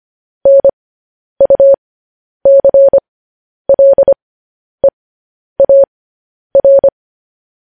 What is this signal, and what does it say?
Morse code, heard as a keyed tone, about 25 wpm, spaced as for 10 wpm, 550 Hz. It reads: NUCLEAR
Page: news headlines in morse code at 10 wpm